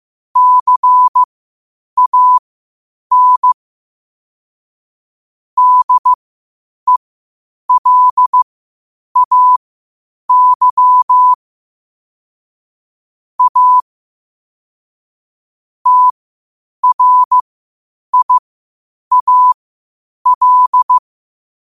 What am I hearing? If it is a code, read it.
Morse code, heard as a keyed tone, about 15 wpm, spaced as for 8 wpm, 1000 Hz. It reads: CAN DELAY A TRIAL
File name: qotdfiles / 8wpm